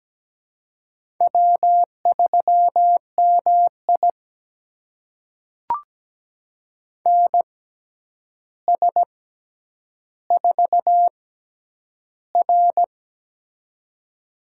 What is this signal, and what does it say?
Morse code, heard as a keyed tone, about 17 wpm, spaced as for 9 wpm, 700 Hz. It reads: W3MI NS4R
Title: USA Call Signs - 4 Characters - 6x Character Spacing - 17wpm